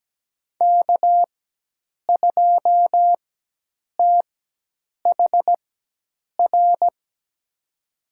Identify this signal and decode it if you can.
Morse code, heard as a keyed tone, about 17 wpm, spaced as for 8 wpm, 700 Hz. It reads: K2THR
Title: USA Call Signs - 5 Characters - 4x Character Spacing - 17wpm